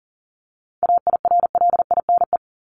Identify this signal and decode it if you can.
Morse code, heard as a keyed tone, about 40 wpm, 700 Hz. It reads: AIRLINE
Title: Top 1501-1750 Words 40wpm